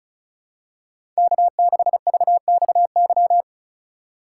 Morse code, heard as a keyed tone, about 36 wpm, 700 Hz. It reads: K6VXY
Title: USA Call Signs - 5 Characters - 2x Character Spacing - 35wpm